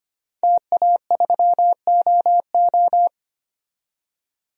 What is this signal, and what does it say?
Morse code, heard as a keyed tone, about 25 wpm, 700 Hz. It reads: TA3OO
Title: DXCC-100 Call Signs - 5 Characters - 25wpm